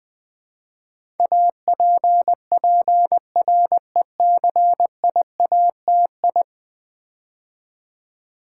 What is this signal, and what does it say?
Morse code, heard as a keyed tone, about 20 wpm, 700 Hz. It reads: APPRECIATI
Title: Build A Word - 5 Letters At A Time - 12-Letter Words - 20wpm